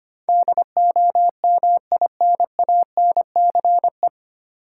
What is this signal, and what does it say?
Morse code, heard as a keyed tone, about 25 wpm, 700 Hz. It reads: DOMINANCE
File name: practice / 25wpm / 9 Letter Words - First 3 Letters Given - 25wpm